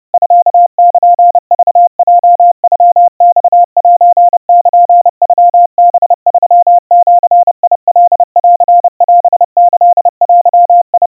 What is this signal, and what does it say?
Morse code (Japanese Wabun code), heard as a keyed tone, about 30 wpm, 700 Hz. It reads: ミルクヲノマセルノハ3ジカンオキデ
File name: 110 cpm 700 Hz Crispy